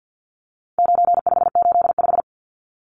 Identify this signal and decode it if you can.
Morse code, heard as a keyed tone, about 50 wpm, 700 Hz. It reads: 9585
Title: Numbers - 4-Digit 50wpm